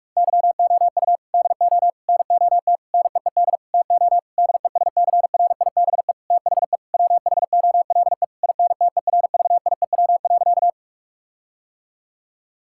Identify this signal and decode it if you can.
Morse code, heard as a keyed tone, about 46 wpm, 700 Hz. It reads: YOU DO NOT NEED TO DESCRIBE THE WHOLE INTERVIEW.
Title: Sentences from Top 1600 Words 45wpm